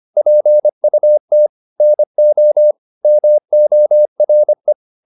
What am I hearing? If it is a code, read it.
Morse code, heard as a keyed tone, about 25 wpm, 600 Hz. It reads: PUT NO MORE
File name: the old man and the sea part 1 25wpm